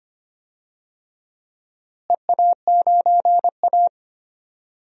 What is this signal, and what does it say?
Morse code, heard as a keyed tone, about 25 wpm, 700 Hz. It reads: EA9A
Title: DXCC-100 Call Signs - 4 Characters - Repeated 2x - 25wpm